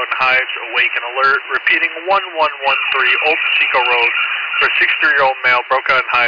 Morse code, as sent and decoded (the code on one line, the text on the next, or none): none